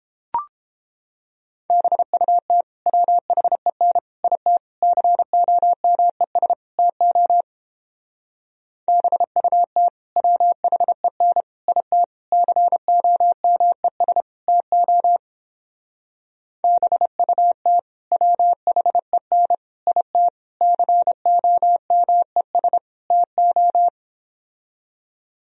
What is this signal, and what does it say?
Morse code, heard as a keyed tone, about 29 wpm, 700 Hz. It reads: BUT WHEN IT COMES TO BUT WHEN IT COMES TO BUT WHEN IT COMES TO